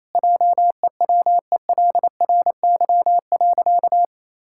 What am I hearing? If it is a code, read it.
Morse code, heard as a keyed tone, about 27 wpm, 700 Hz. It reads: JEWELRY.
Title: Pangram Challenge v2 - 28wpm